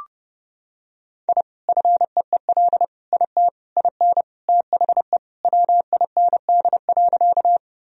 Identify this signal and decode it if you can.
Morse code, heard as a keyed tone, about 30 wpm, 700 Hz. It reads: I FEEL IT IN THE WIND.
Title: Sentences from Top 1000 Words - Review Part 2 - 30wpm